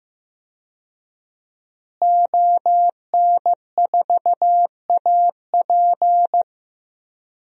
Morse code, heard as a keyed tone, about 16 wpm, 700 Hz. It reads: ON4AP
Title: DXCC-100 Call Signs - 5 Characters - Repeated 2x - 15wpm